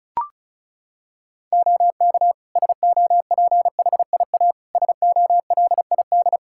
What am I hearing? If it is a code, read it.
Morse code, heard as a keyed tone, about 34 wpm, 700 Hz. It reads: OK SOPHIA SOLID
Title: Practice QSO - Exchange 3 - More Info - 35wpm